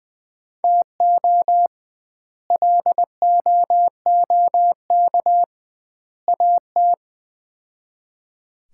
Morse code, heard as a keyed tone, about 20 wpm, 700 Hz. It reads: TO LOOK AT
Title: 2x Word-Spacing - Sets of 3 Words - 20wpm